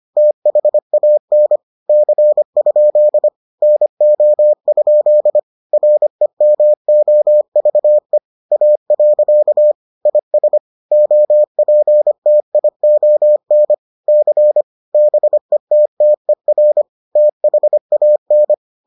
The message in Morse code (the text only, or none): THAN C? NO? REMOVE A. IS OPTION C BETTER THAN